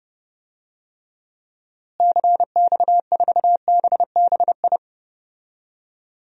Morse code, heard as a keyed tone, about 32 wpm, 700 Hz. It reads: CX4BBI
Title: DXCC-100 Call Signs - 6 Characters - Repeated 2x - 30wpm